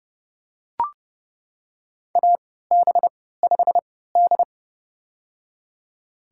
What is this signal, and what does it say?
Morse code, heard as a keyed tone, about 30 wpm, spaced as for 17 wpm, 700 Hz. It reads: AB5D
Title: USA Call Signs - 3x Character Spacing - 30wpm